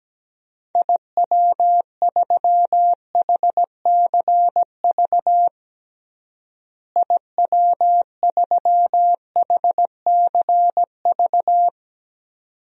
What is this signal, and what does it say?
Morse code, heard as a keyed tone, about 17 wpm, 700 Hz. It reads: IW3HCV IW3HCV
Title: DXCC-100 Call Signs - 6 Characters - Repeated 2x - 17wpm